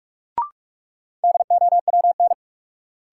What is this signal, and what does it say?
Morse code, heard as a keyed tone, about 46 wpm, 700 Hz. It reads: DOWN
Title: Sentences from Top 300 Words - New Word Drill - 45wpm